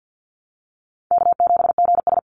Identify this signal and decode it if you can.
Morse code, heard as a keyed tone, about 50 wpm, 700 Hz. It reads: K7GS
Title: Easy Call Signs - 50wpm